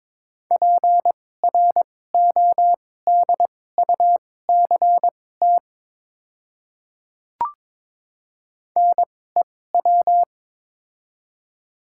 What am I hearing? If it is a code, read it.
Morse code, heard as a keyed tone, about 22 wpm, spaced as for 16 wpm, 690 Hz. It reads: PRODUCT NEW
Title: 2x Character Spacing - Top 500 Words - 22wpm